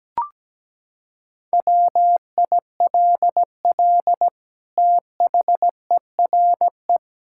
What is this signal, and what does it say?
Morse code, heard as a keyed tone, about 17 wpm, 700 Hz. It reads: WILL THERE
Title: Sentences from Top 1600 Words 17wpm